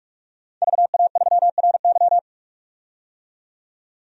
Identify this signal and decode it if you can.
Morse code, heard as a keyed tone, about 46 wpm, 690 Hz. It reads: VA3RY
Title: DXCC-100 Call Signs - 5 Characters - 45wpm